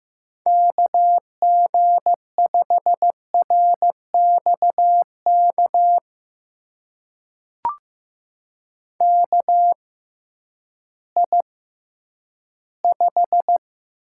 Morse code, heard as a keyed tone, about 15 wpm, spaced as for 10 wpm, 700 Hz. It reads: KG5RXK KI5
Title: USA Call Signs - 6 Characters - 6x Character Spacing - 15wpm